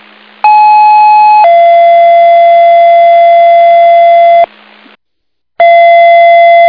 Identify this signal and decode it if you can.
Two-tone page, heard as a one-second tone, then a three-second tone, about 850 then 680 Hz. It is heard from 0.4 to 4.5 s.